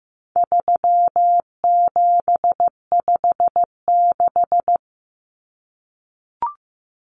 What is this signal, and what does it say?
Morse code, heard as a keyed tone, about 15 wpm, 700 Hz. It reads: 3756
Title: Common Cut Numbers - 4-Digit 15wpm